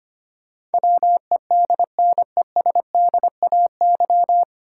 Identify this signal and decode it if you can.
Morse code, heard as a keyed tone, about 25 wpm, 700 Hz. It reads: WEDNESDAY
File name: Nine Letter Words - 25wpm